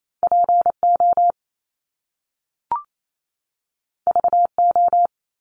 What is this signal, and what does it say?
Morse code, heard as a keyed tone, about 28 wpm, 700 Hz. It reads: PO VO
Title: Top Two Letter Combinations in Words 28wpm